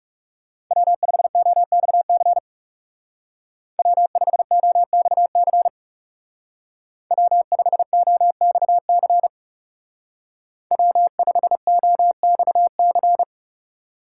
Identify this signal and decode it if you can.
Morse code, heard as a keyed tone, about 35 wpm, 700 Hz. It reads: W5OXC W5OXC W5OXC W5OXC
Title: Speed Racing - USA 1x3 Call Signs 30wpm